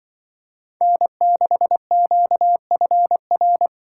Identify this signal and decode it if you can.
Morse code, heard as a keyed tone, about 24 wpm, 700 Hz. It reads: N6QFR
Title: Speed Racing - USA 1x3 Call Signs 20wpm